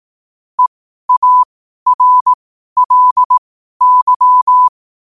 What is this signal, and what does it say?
Morse code, heard as a keyed tone, about 18 wpm, spaced as for 12 wpm, 1000 Hz. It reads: EARLY
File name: qotdfiles / 12wpm